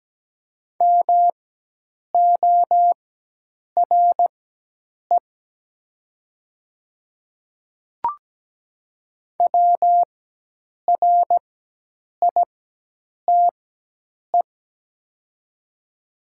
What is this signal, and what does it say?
Morse code, heard as a keyed tone, about 17 wpm, spaced as for 8 wpm, 700 Hz. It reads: MORE WRITE
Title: 4x Character Spacing - Top 100 Words - 17wpm